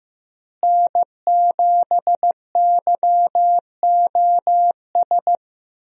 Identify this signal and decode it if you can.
Morse code, heard as a keyed tone, about 15 wpm, 700 Hz. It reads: N7YOS
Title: USA Call Signs - 5 Characters - Repeated 2x - 15wpm